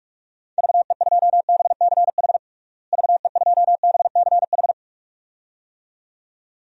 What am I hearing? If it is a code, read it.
Morse code, heard as a keyed tone, about 46 wpm, 700 Hz. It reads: VE2BCH VE2BCH
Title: DXCC-100 Call Signs - 6 Characters - Repeated 2x - 45wpm